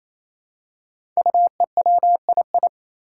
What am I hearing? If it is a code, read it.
Morse code, heard as a keyed tone, about 28 wpm, 700 Hz. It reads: UEWII
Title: Five Challenging Letter-Number Combinations 28wpm